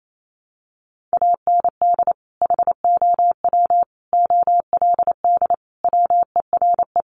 Morse code, heard as a keyed tone, about 28 wpm, 700 Hz. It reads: AND HOW OLD WERE
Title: Sentences from Top 200 Words 28wpm